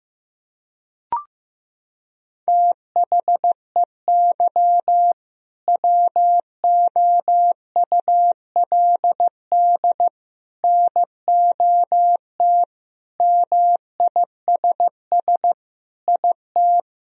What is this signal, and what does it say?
Morse code, heard as a keyed tone, about 15 wpm, 700 Hz. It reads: THEY WOULD NOT MISS IT
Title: Sentences from Top 1000 Words - Review Part 3 - 15wpm